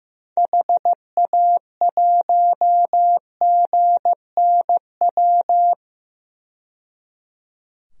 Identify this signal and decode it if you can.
Morse code, heard as a keyed tone, about 15 wpm, 700 Hz. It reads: HA1GNW
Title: DXCC-100 Call Signs - 6 Characters - 15wpm